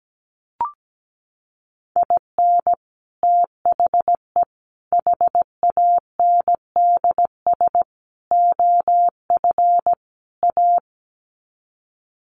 Morse code, heard as a keyed tone, about 17 wpm, 700 Hz. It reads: IN THE HANDS OF A
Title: Sets of 5 Words Encore 17wpm